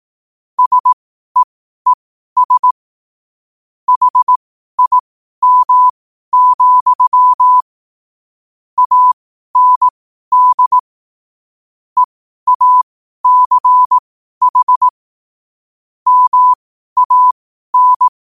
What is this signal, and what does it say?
Morse code, heard as a keyed tone, about 18 wpm, spaced as for 12 wpm, 1000 Hz. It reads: SEES HIM, AND EACH MAN
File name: qotdfiles / 12wpm